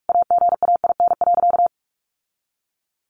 Morse code, heard as a keyed tone, about 44 wpm, 700 Hz. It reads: AGAIN.
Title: Sentences from Top 1300 Words 45wpm